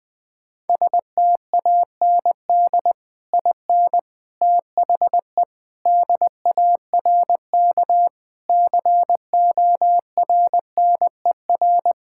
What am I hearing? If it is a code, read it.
Morse code, heard as a keyed tone, about 21 wpm, 700 Hz. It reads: STAND IN THE DARK CORNER
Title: Sentences from Top 1600 Words 20wpm